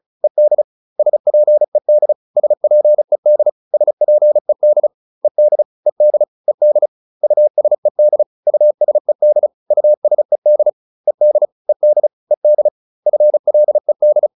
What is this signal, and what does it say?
Morse code, heard as a keyed tone, about 34 wpm, 600 Hz. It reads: ED SPED SPED SPED ED ED ED USED USED USED ED ED ED FLED